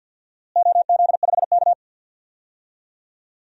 Morse code, heard as a keyed tone, about 52 wpm, 700 Hz. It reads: OZ5K
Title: DXCC-100 Call Signs - 4 Characters - Repeated 2x - 50wpm